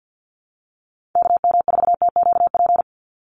Morse code, heard as a keyed tone, about 50 wpm, 700 Hz. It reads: KM4TQP
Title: USA Call Signs 50wpm